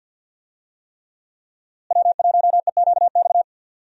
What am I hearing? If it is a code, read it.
Morse code, heard as a keyed tone, about 47 wpm, 700 Hz. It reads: W1EQX
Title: USA Call Signs - 5 Characters - Repeated 3x - 50wpm